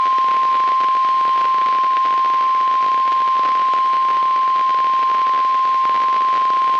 Emergency Alert System (SAME) tones (the 1050 Hz attention tone), with noise. It is heard from 0.3 to 5.7 s.
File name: eas messages / EAS WXL38 Binghamton